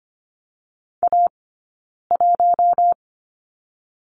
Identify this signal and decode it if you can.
Morse code, heard as a keyed tone, about 24 wpm, 700 Hz. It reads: A 1
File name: Lesson 15d - Number 5 - Two Characters 25wpm